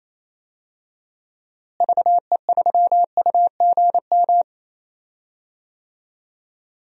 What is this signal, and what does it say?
Morse code, heard as a keyed tone, about 28 wpm, 700 Hz. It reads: VE3UGM
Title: DXCC-100 Call Signs - 6 Characters - Repeated 2x - 28wpm